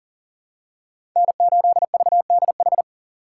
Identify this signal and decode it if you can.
Morse code, heard as a keyed tone, about 40 wpm, 700 Hz. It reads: N8VDH
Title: USA Call Signs - 5 Characters - 6x Character Spacing - 40wpm